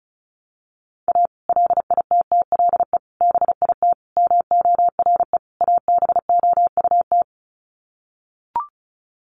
Morse code, heard as a keyed tone, about 34 wpm, 700 Hz. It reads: A LITTLE BIT MORE ABOUT
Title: Sets of 5 Words Encore2 35wpm